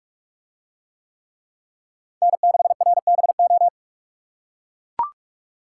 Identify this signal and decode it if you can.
Morse code, heard as a keyed tone, about 41 wpm, 700 Hz. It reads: N6RBO